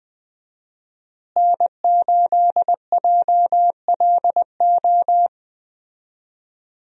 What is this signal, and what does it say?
Morse code, heard as a keyed tone, about 20 wpm, 700 Hz. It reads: N8JLO
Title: USA Call Signs - 5 Characters - Repeated 3x - 20wpm